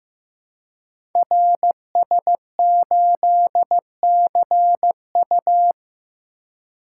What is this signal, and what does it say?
Morse code, heard as a keyed tone, about 15 wpm, 700 Hz. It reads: RS8CU
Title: Code Groups - 5 Letter:Numbers - 15wpm